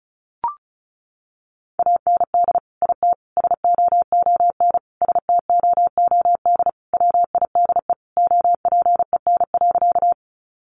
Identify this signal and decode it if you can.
Morse code, heard as a keyed tone, about 34 wpm, 700 Hz. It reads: AND IT SOON STOOD WIDE OPEN.
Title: Sentences from Top 900 Words 35wpm